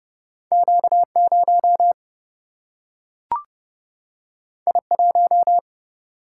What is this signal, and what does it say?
Morse code, heard as a keyed tone, about 30 wpm, 700 Hz. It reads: Q0 I1
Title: Single-Letter Single-Number 30wpm